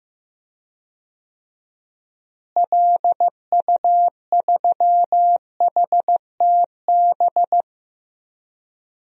Morse code, heard as a keyed tone, about 15 wpm, 700 Hz. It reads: LU3HTB
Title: DXCC-100 Call Signs - 6 Characters - Repeated 3x - 15wpm